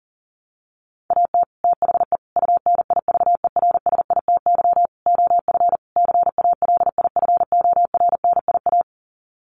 Morse code, heard as a keyed tone, about 40 wpm, 700 Hz. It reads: AT THE UNIVERSITY OF CALIFORNIA
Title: Sets of 5 Words Encore2 40wpm